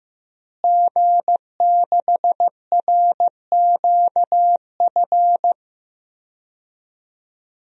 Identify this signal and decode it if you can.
Morse code, heard as a keyed tone, about 15 wpm, 700 Hz. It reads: G6RQF
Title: DXCC-100 Call Signs - 5 Characters - Repeated 3x - 15wpm